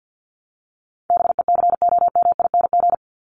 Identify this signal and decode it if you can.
Morse code, heard as a keyed tone, about 50 wpm, 700 Hz. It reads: BECOMING
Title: ING Suffix - Top 100 Words with Base Words 50wpm